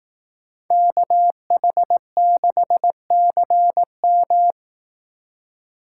Morse code, heard as a keyed tone, about 18 wpm, 700 Hz. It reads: KH6CM
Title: Speed Racing - USA 2x2 Call Signs 15wpm